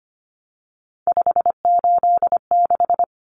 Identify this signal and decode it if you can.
Morse code, heard as a keyed tone, about 25 wpm, 700 Hz. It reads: 586
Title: Common Cut Numbers - 3-Digit 25wpm